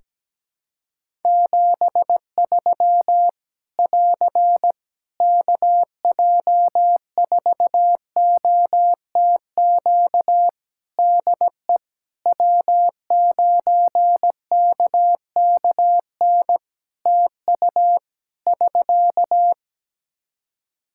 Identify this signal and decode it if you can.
Morse code, heard as a keyed tone, about 17 wpm, 700 Hz. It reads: 73 <AR> KJ4OTQ DE W9KKN TU <SK>
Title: Practice QSO - Exchange 4 - Ending - 17wpm